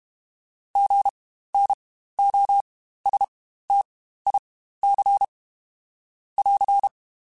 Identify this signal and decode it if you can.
Morse code, heard as a keyed tone, about 32 wpm, spaced as for 15 wpm, 800 Hz. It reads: GNOSTIC <AR>
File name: Spaced Alphas / Spaced Alphas mp3s / Gs sp @30